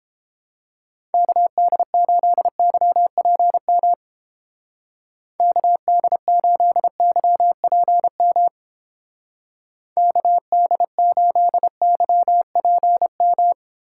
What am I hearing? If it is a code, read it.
Morse code, heard as a keyed tone, about 29 wpm, 700 Hz. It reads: KD8YPM KD8YPM KD8YPM